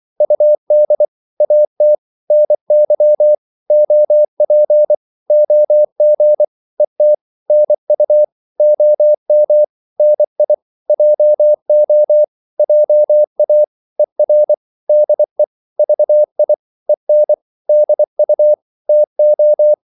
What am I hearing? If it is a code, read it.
Morse code, heard as a keyed tone, about 24 wpm, 600 Hz. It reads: UD AT NY OP OG ET NU OM NI JO JA ER DE VI EN DU TO
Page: Korte danske ord 24wpm